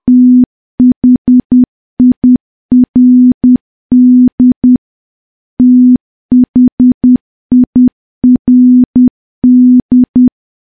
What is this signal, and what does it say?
Morse code, heard as a keyed tone, about 10 wpm, 250 Hz. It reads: THIRD THIRD
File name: morse / morse-training-10WPM